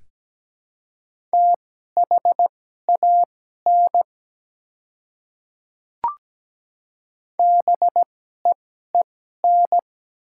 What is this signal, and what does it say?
Morse code, heard as a keyed tone, about 17 wpm, spaced as for 12 wpm, 690 Hz. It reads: THAN BEEN
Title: 2x Character Spacing - Top 100 Words - 17wpm